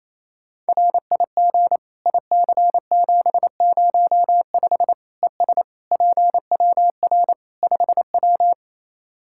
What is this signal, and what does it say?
Morse code, heard as a keyed tone, about 28 wpm, 700 Hz. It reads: RIG IC705 ES PWR 5W
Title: Practice QSO - Exchange 2 - Info - 28wpm